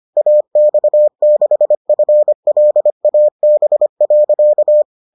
Morse code, heard as a keyed tone, about 25 wpm, 600 Hz. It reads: AX6FLAB.